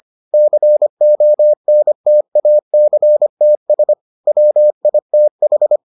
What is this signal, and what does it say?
Morse code, heard as a keyed tone, about 25 wpm, 600 Hz. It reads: CONTACTS WITH